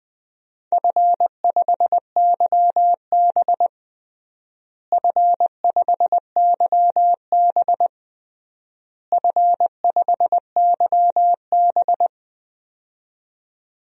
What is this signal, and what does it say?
Morse code, heard as a keyed tone, about 20 wpm, 700 Hz. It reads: F5YB F5YB F5YB
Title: DXCC-100 Call Signs - 4 Characters - Repeated 3x - 20wpm